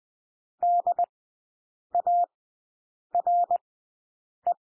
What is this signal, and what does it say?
Morse code, heard as a keyed tone, about 20 wpm, spaced as for 8 wpm, 700 Hz. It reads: DARE
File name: cwmp3 / W1 SOUND WORDS 20 wpm